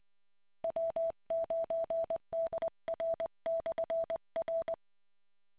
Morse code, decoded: W9DR/R